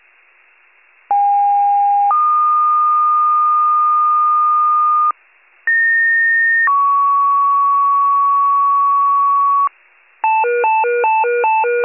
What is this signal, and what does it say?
Two-tone page, heard as a one-second tone, then a three-second tone, about 810 then 1220 Hz, about 1850 then 1120 Hz. It is heard from 1.1 to 9.7 s.